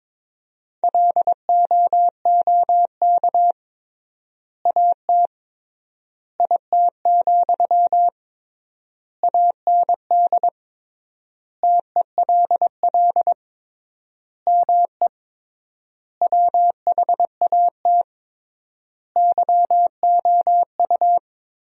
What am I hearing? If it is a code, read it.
Morse code, heard as a keyed tone, about 22 wpm, 700 Hz. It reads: LOOK AT IT, AND TELL ME WHAT YOU
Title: 3x Word-Spacing - Sentences from Top 200 Words - 22wpm